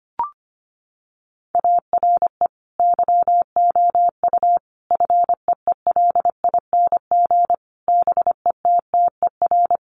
Morse code, heard as a keyed tone, about 25 wpm, 700 Hz. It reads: ARE YOU FEELING BETTER
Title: Sentences from Top 1200 Words 25wpm